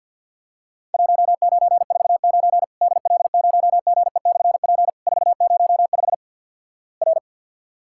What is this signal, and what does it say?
Morse code, heard as a keyed tone, about 46 wpm, 700 Hz. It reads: 1949 DL0GE/P 405 R
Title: Worked All Europe DX Contest - QTC Exchange - 50wpm